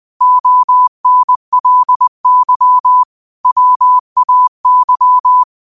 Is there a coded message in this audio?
Yes: Morse code from 0.2 to 5.4 s, heard as a keyed tone, about 20 wpm, 1000 Hz. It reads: ONLY WAY